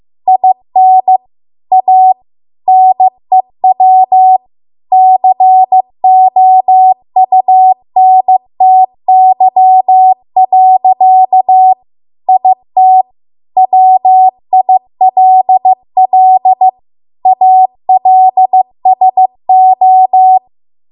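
Morse code, heard as a keyed tone, about 15 wpm, 750 Hz. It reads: IN A NEW COUNTY. IT WILL ALSO
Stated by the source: Morse code practice at 15 wpm from the national amateur radio association's code archive.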